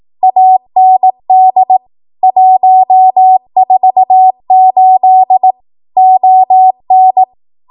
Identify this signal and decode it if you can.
Morse code, heard as a keyed tone, about 18 wpm, 750 Hz. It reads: AND 148 ON
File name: Archive / 18 WPM / 18WPM